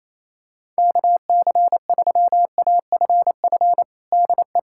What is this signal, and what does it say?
Morse code, heard as a keyed tone, about 28 wpm, 700 Hz. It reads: KC3AFF DE
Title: USA Callsign DE Callsign - 6 Characters - 28wpm